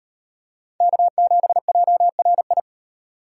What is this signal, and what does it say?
Morse code, heard as a keyed tone, about 38 wpm, 700 Hz. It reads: K7JRI